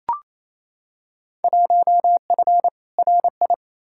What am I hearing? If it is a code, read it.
Morse code, heard as a keyed tone, about 28 wpm, 700 Hz. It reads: 1F RI
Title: Field Day Exchanges Encore - 28wpm